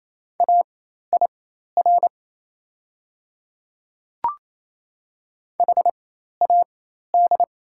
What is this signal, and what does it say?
Morse code, heard as a keyed tone, about 28 wpm, spaced as for 13 wpm, 700 Hz. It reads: AIR HAD
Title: 4x Character Spacing - Top 200 Words - 28wpm